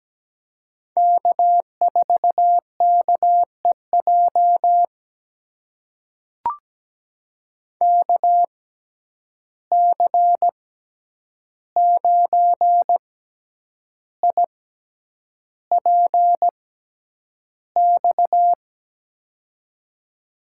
Morse code, heard as a keyed tone, about 17 wpm, spaced as for 8 wpm, 700 Hz. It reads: K4KEJ KC9IPX
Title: USA Call Signs - 6x Character Spacing - 17wpm